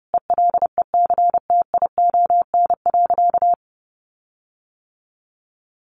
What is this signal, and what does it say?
Morse code, heard as a keyed tone, about 30 wpm, 700 Hz. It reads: ELECTION.
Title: Sentences from Top 1500 Words 30wpm